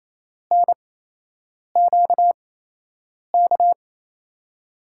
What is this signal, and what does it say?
Morse code, heard as a keyed tone, about 27 wpm, spaced as for 8 wpm, 700 Hz. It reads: NQK
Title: USA Call Signs - 6 Characters - 8x Character Spacing - 28wpm